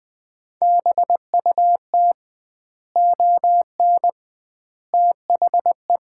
Morse code, heard as a keyed tone, about 20 wpm, 700 Hz. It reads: BUT ON THE
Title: 2x Word-Spacing - Sets of 5 Words - 20wpm